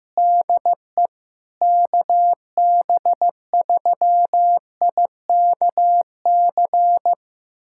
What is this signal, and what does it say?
Morse code, heard as a keyed tone, about 16 wpm, 700 Hz. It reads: DE KB3IKC
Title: USA Callsign DE Callsign 15wpm